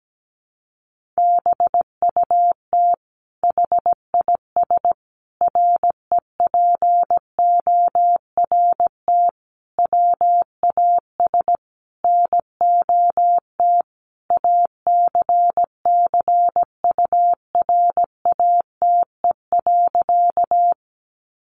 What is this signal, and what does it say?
Morse code, heard as a keyed tone, about 17 wpm, 700 Hz. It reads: BUT HIS REPORT WAS NOT ACCURATE.
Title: Sentences from Top 1200 Words 17wpm